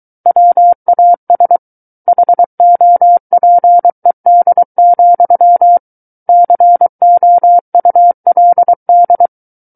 Morse code, heard as a keyed tone, about 23 wpm, 700 Hz. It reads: WAS HOPED, COULD
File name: Book One / 14-Chapter 14 - 23 WPM